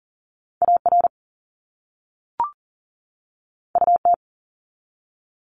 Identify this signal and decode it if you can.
Morse code, heard as a keyed tone, about 40 wpm, 700 Hz. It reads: AR UT
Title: ARRL Sections 40wpm